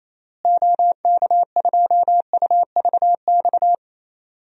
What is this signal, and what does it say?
Morse code, heard as a keyed tone, about 28 wpm, 700 Hz. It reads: OK2UVX
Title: DXCC-100 Call Signs - 6 Characters - Repeated 3x - 28wpm